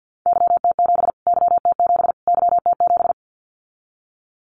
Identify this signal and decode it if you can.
Morse code, heard as a keyed tone, about 50 wpm, 700 Hz. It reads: YT7 YT7 YT7
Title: Callsign Prefixes for Top 25 DXCC - Repeated 3x - 50wpm